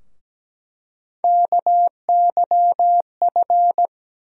Morse code, heard as a keyed tone, about 17 wpm, 700 Hz. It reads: KYF
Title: USA Call Sign Postfixes - 3 Letters 17wpm